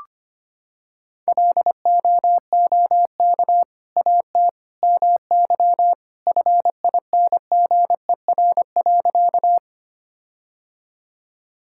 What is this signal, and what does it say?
Morse code, heard as a keyed tone, about 25 wpm, 700 Hz. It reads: LOOK AT MY FINGER.
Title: Sentences from Top 1600 Words 25wpm